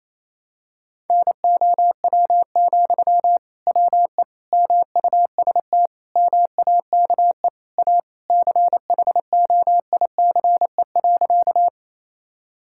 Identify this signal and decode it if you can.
Morse code, heard as a keyed tone, about 28 wpm, 700 Hz. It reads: NOW, WE MUST MAKE A CHOICE.
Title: Sentences from Top 1000 Words - Review Part 6 - 28wpm